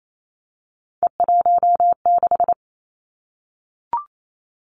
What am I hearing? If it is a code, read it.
Morse code, heard as a keyed tone, about 28 wpm, 700 Hz. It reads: E16
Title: Three Challenging Letter-Number Combinations 28wpm